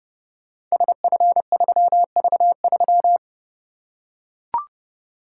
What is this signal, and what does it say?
Morse code, heard as a keyed tone, about 30 wpm, 700 Hz. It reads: SF3V3
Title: Code Groups - 5 Letter:Numbers - 30wpm